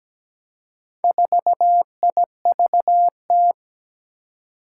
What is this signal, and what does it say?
Morse code, heard as a keyed tone, about 17 wpm, 700 Hz. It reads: 4IVT
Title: Four Challenging Letter-Number Combinations 17wpm